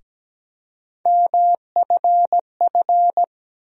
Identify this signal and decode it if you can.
Morse code, heard as a keyed tone, about 17 wpm, 700 Hz. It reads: MFF
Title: USA Call Sign Postfixes - 3 Letters 17wpm